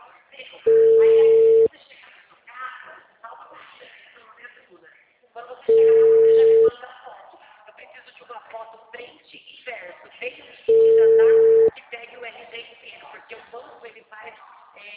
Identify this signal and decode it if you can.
Ringback tone (China, 450 Hz), with speech.